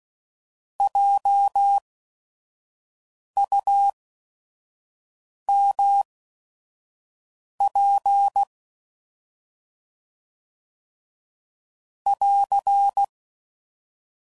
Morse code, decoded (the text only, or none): JUMP <AR>